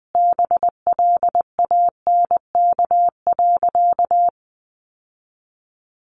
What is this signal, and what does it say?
Morse code, heard as a keyed tone, about 20 wpm, 700 Hz. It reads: BLANK.
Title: Sentences from Top 1500 Words 20wpm